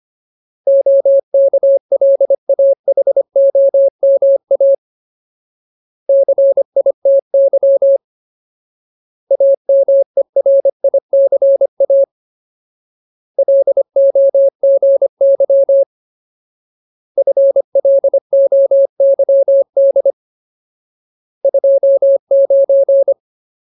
Morse code, heard as a keyed tone, about 25 wpm, 550 Hz. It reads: OKLAHOMA CITY AMERICA LOGY FLOYD 29
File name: Ultimate Word List / 25WPM Ultimate